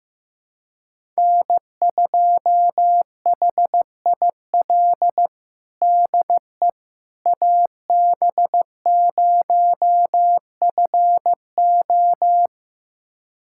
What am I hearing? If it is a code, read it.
Morse code, heard as a keyed tone, about 16 wpm, 700 Hz. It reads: N2HIL DE AB0FO
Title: USA Callsign DE Callsign - 5 Characters - 15wpm